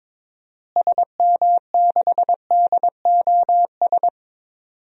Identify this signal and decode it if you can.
Morse code, heard as a keyed tone, about 22 wpm, 700 Hz. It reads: SM6DOS
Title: DXCC-100 Call Signs - 6 Characters - Repeated 2x - 22wpm